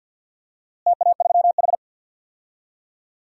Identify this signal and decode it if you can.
Morse code, heard as a keyed tone, about 51 wpm, 700 Hz. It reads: TA3H
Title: DXCC-100 Call Signs - 4 Characters - Repeated 2x - 50wpm